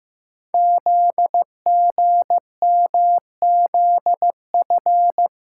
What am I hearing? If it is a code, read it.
Morse code, heard as a keyed tone, about 15 wpm, 700 Hz. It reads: ZGMZF
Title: Code Groups - 5 Letters - 15wpm